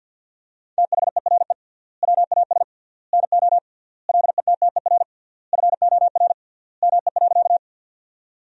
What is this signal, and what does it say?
Morse code, heard as a keyed tone, about 47 wpm, 700 Hz. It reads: THERE WAS NO LETTER FOR ME.
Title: 3x Word-Spacing - Sentences from Top 300 Words - 50wpm